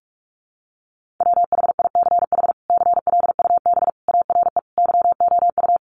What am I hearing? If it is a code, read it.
Morse code, heard as a keyed tone, about 44 wpm, 700 Hz. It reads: WHICH CLUB ARE YOU
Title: Sentences from Top 1200 Words 45wpm